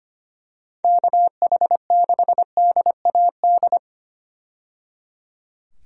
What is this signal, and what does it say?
Morse code, heard as a keyed tone, about 25 wpm, 700 Hz. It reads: KH6DAD
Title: DXCC-100 Call Signs - 6 Characters - 25wpm